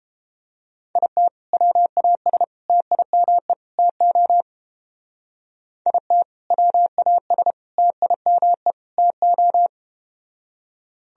Morse code, heard as a keyed tone, about 30 wpm, 700 Hz. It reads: IT WAS TIME TO IT WAS TIME TO